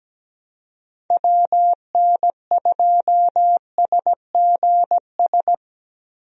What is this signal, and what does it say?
Morse code, heard as a keyed tone, about 17 wpm, 700 Hz. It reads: WN2SGS
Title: USA Call Signs - 6 Characters - Repeated 3x - 17wpm